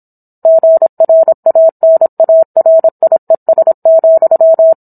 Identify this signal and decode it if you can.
Morse code, heard as a keyed tone, about 27 wpm, 650 Hz. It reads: GRANARIES,